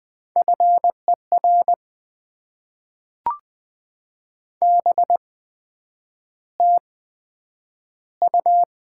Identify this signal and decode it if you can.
Morse code, heard as a keyed tone, about 20 wpm, spaced as for 9 wpm, 700 Hz. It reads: FER BTU
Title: Top 20 QSO Elements - 8x Character Spacing - 20wpm